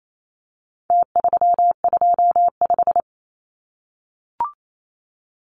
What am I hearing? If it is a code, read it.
Morse code, heard as a keyed tone, about 28 wpm, 700 Hz. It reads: T325
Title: Numbers - Contest Style - 4-Digit 28wpm